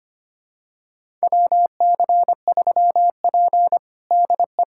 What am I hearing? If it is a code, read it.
Morse code, heard as a keyed tone, about 25 wpm, 700 Hz. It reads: WC3P DE
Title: USA Callsign DE Callsign - 4 Characters - 25wpm